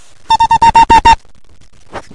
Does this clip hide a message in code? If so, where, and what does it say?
No.